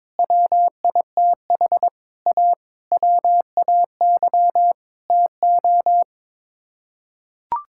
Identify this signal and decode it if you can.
Morse code, heard as a keyed tone, about 22 wpm, 700 Hz. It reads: WITH A WAY TO
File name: practice / 22wpm / Sets of 7 Words - 22wpm